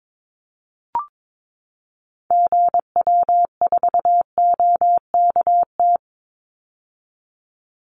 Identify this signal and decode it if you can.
Morse code, heard as a keyed tone, about 22 wpm, 700 Hz. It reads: GW4OKT
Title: Call Signs - 22wpm